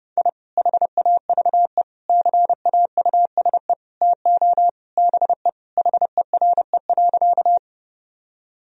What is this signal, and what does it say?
Morse code, heard as a keyed tone, about 30 wpm, 700 Hz. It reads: I HAVE CAUSE TO BE HERE.
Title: Sentences from Top 1000 Words - Review Part 1 - 30wpm